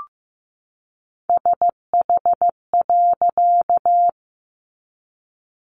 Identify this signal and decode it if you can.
Morse code, heard as a keyed tone, about 15 wpm, 700 Hz. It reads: SH.
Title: A Sound Of Thunder 15wpm